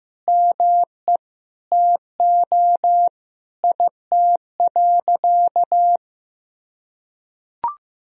Morse code, heard as a keyed tone, about 15 wpm, 700 Hz. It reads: ME TO IT.
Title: Sentences from Top 1000 Words - Review Part 2 - 15wpm